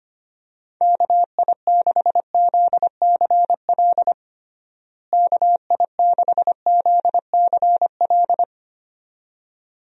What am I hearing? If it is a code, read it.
Morse code, heard as a keyed tone, about 25 wpm, 700 Hz. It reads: KI6ZCL KI6ZCL
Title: USA Call Signs - 6 Characters - Repeated 2x - 25wpm